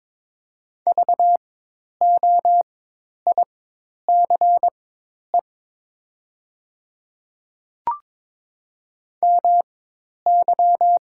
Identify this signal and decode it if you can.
Morse code, heard as a keyed tone, about 22 wpm, spaced as for 10 wpm, 700 Hz. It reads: VOICE MY
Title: 4x Character Spacing - Top 500 Words - 22wpm